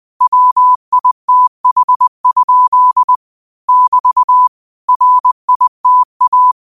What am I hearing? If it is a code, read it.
Morse code, heard as a keyed tone, about 20 wpm, 1000 Hz. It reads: WITH? <BT> RITA